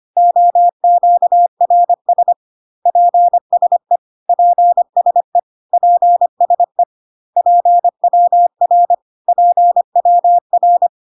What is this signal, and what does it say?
Morse code, heard as a keyed tone, about 25 wpm, 700 Hz. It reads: OQRS PSE PSE PSE PWR PWR
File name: cw / CW common abbreviations alphabetical repeated 3 times 25 25wpm ik1hge